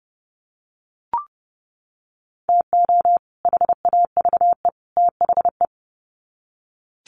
Morse code, heard as a keyed tone, about 30 wpm, 700 Hz. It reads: TO HAVE THE
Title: Sets of 3 Words 30wpm